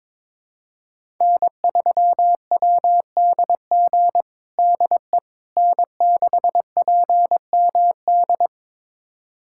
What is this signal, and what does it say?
Morse code, heard as a keyed tone, about 22 wpm, 700 Hz. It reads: N3WDG DE N6PMD
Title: USA Callsign DE Callsign - 5 Characters - 22wpm